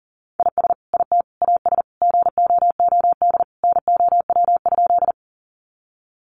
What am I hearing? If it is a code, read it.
Morse code, heard as a keyed tone, about 40 wpm, 700 Hz. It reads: IS IT AS GOOD NOW?
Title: Sentences from Top 200 Words 40wpm